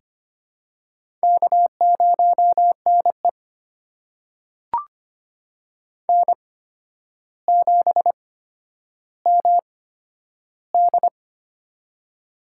Morse code, heard as a keyed tone, about 25 wpm, spaced as for 11 wpm, 700 Hz. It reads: K0NE N7MD
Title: USA Call Signs - 4 Characters - 8x Character Spacing - 25wpm